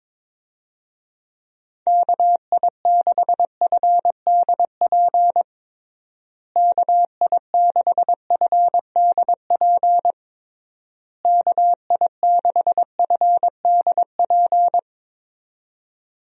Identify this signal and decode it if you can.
Morse code, heard as a keyed tone, about 22 wpm, 700 Hz. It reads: KI6FDP KI6FDP KI6FDP
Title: USA Call Signs - 6 Characters - Repeated 3x - 22wpm